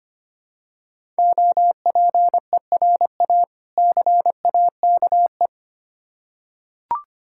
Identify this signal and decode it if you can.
Morse code, heard as a keyed tone, about 25 wpm, 700 Hz. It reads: OPERA CAKE
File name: practice / 25wpm / Difficult Words with Hints - 25wpm